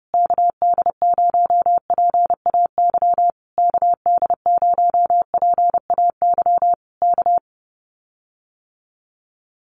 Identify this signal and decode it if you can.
Morse code, heard as a keyed tone, about 30 wpm, 700 Hz. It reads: KD0PAY KD0PAY K
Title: Common QSO Phrases Encore 30wpm